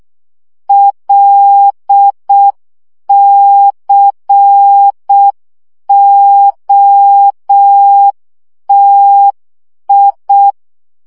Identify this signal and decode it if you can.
Morse code, heard as a keyed tone, about 6 wpm, 800 Hz. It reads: LCOTI
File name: morze vizsga 2011-3 5